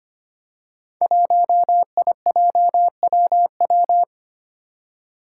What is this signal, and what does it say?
Morse code, heard as a keyed tone, about 25 wpm, 700 Hz. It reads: 1IJWW
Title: Five Challenging Letter-Number Combinations 25wpm